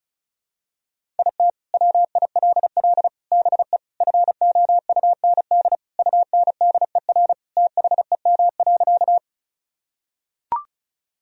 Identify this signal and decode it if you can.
Morse code, heard as a keyed tone, about 36 wpm, 700 Hz. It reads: IT WILL BE FOUND UNDER THEM.
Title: Sentences from Top 1000 Words - Review Part 1 - 35wpm